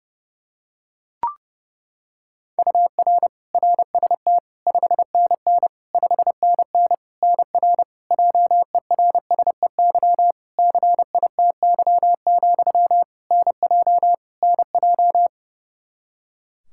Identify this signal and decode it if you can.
Morse code, heard as a keyed tone, about 30 wpm, 700 Hz. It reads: UR RST 5NN 5NN NR JERSEY CITY, NJ NJ
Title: Practice QSO - Exchange 1 - Beginning - 30wpm